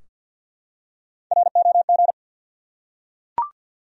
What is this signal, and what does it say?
Morse code, heard as a keyed tone, about 53 wpm, 700 Hz. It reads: ROG
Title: USA Common Names - 50wpm